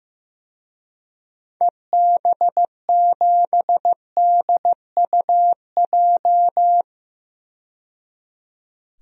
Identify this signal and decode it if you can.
Morse code, heard as a keyed tone, about 16 wpm, 700 Hz. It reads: EB7DUJ
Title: DXCC-100 Call Signs - 6 Characters - 15wpm